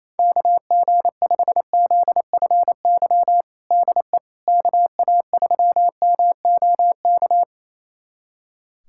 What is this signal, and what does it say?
Morse code, heard as a keyed tone, about 28 wpm, 700 Hz. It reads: KG5ZFY DE KA3MOK
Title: USA Callsign DE Callsign - 6 Characters - 28wpm